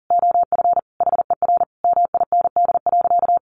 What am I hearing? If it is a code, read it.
Morse code, heard as a keyed tone, about 40 wpm, 700 Hz. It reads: OF HER MIND.
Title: Sentences from Top 1300 Words 40wpm